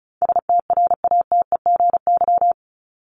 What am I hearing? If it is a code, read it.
Morse code, heard as a keyed tone, about 34 wpm, 700 Hz. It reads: STRATEGY
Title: Top 501-750 Words 35wpm